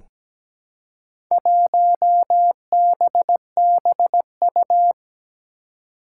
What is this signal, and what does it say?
Morse code, heard as a keyed tone, about 17 wpm, 700 Hz. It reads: 1BBU
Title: Four Challenging Letter-Number Combinations 17wpm